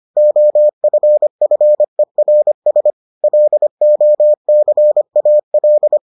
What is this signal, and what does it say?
Morse code, heard as a keyed tone, about 25 wpm, 600 Hz. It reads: OFFERS LOCAL